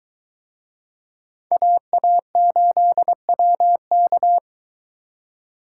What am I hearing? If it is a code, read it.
Morse code, heard as a keyed tone, about 23 wpm, 700 Hz. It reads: AA8WK